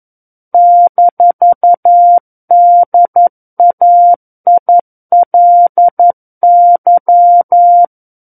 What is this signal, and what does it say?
Morse code, heard as a keyed tone, about 11 wpm, 700 Hz. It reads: -DAILY
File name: 11 WPM